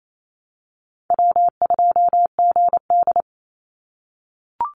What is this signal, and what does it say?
Morse code, heard as a keyed tone, about 28 wpm, 700 Hz. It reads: W2GD
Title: Call Signs - 28wpm